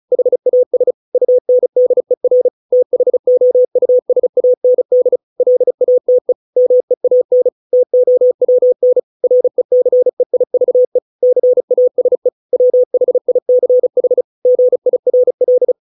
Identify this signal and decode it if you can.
Morse code, heard as a keyed tone, about 34 wpm, 500 Hz. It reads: HAS UNDER THOUSAND LATE MEAN TOWN RECEIVE CASE WHICH GIRL
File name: 632 words / 632 Words 35WPM